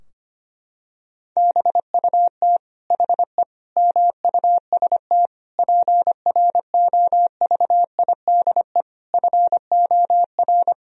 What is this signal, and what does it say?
Morse code, heard as a keyed tone, about 25 wpm, 700 Hz. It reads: BUT HE MUST PROVIDE FOR
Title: Sentences from Top 1000 Words - Review Part 3 - 25wpm